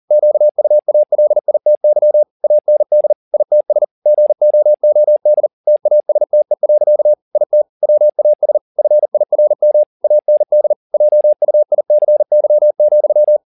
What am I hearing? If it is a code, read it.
Morse code, heard as a keyed tone, about 40 wpm, 600 Hz. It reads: QUALITY AND ITS GOOD TASTE. IT WAS FIRM AND JUICY,
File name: the old man and the sea 2 40wpm